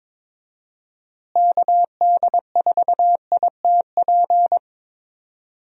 Morse code, heard as a keyed tone, about 22 wpm, 700 Hz. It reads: KD4ITP
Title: USA Call Signs - 6 Characters - Repeated 2x - 22wpm